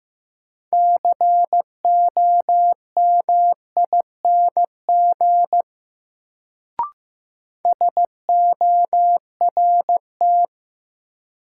Morse code, heard as a keyed tone, about 15 wpm, 700 Hz. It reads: COMING SORT
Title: Sentences from Top 1300 Words - New Word Drill Alt - 15wpm